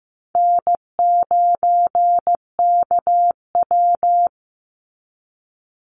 Morse code, heard as a keyed tone, about 15 wpm, 700 Hz. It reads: N9KW
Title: NA SOTA Chaser Callsigns 15wpm